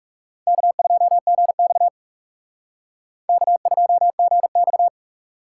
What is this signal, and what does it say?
Morse code, heard as a keyed tone, about 39 wpm, 700 Hz. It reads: K2GX K2GX